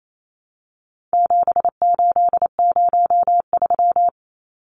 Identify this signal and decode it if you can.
Morse code, heard as a keyed tone, about 28 wpm, 700 Hz. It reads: 7803
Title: Numbers - 4-Digit 28wpm